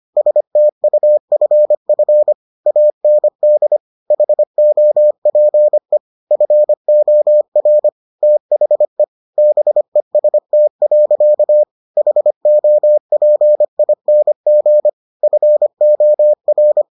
Morse code, heard as a keyed tone, about 25 wpm, 600 Hz. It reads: STUFF AND HOPE FOR THE BEST. HOPING FOR